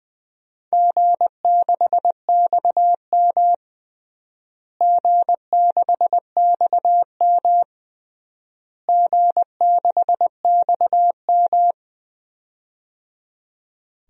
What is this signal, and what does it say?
Morse code, heard as a keyed tone, about 20 wpm, 700 Hz. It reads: G6XM G6XM G6XM
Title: DXCC-100 Call Signs - 4 Characters - Repeated 3x - 20wpm